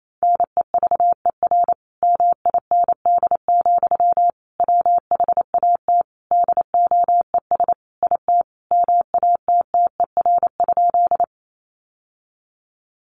Morse code, heard as a keyed tone, about 28 wpm, 700 Hz. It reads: NEVER MIND, WHAT DOES IT MATTER?
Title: Sentences from Top 700 Words 28wpm